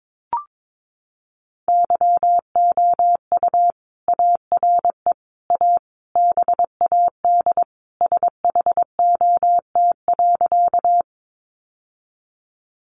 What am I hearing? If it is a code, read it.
Morse code, heard as a keyed tone, about 22 wpm, 700 Hz. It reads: YOU ARE A BAD SHOT.
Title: Sentences from Top 1100 Words 22wpm